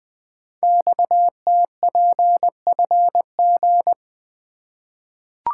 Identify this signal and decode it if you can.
Morse code, heard as a keyed tone, about 20 wpm, 700 Hz. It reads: XTPFG
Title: Code Groups - 5 Letters - 20wpm